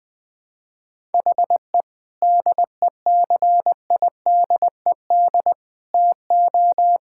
Morse code, heard as a keyed tone, about 21 wpm, 700 Hz. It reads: HE DECIDED TO
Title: Sentences with 3-Word Phrases - 20wpm